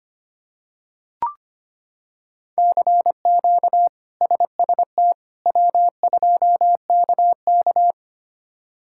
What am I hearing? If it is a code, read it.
Morse code, heard as a keyed tone, about 25 wpm, 700 Hz. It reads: CQ SST W2KK
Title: K1USN SST QSO Practice - 25wpm